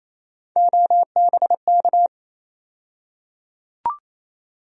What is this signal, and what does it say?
Morse code, heard as a keyed tone, about 28 wpm, 700 Hz. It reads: OBK
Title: USA Call Sign Postfixes - 3 Letters 28wpm